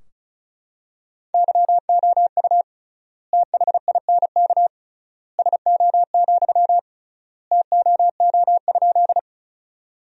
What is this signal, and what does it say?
Morse code, heard as a keyed tone, about 34 wpm, 700 Hz. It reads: YOU THINK SO, TOO?
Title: 3x Word-Spacing - Sentences from Top 200 Words - 35wpm